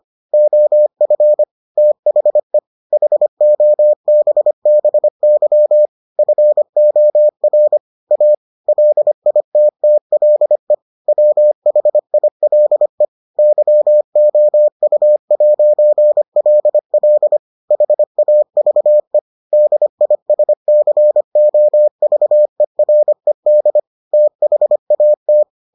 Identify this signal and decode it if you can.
Morse code, heard as a keyed tone, about 25 wpm, 600 Hz. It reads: OF THE HOBBY FOR A LITTLE WHILE YOU'LL HAVE DISCOVERED THAT